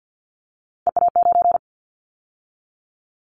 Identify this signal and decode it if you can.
Morse code, heard as a keyed tone, about 50 wpm, 700 Hz. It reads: EA9
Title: Callsign Prefixes for Top 100 DXCC - 50wpm